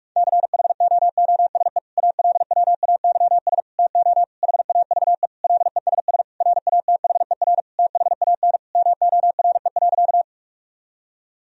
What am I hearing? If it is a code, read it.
Morse code, heard as a keyed tone, about 42 wpm, 700 Hz. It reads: CHOOSE ALWAYS TO HAVE LESS RATHER THAN MORE.
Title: Sentences from Top 1000 Words - Review Part 4 - 45wpm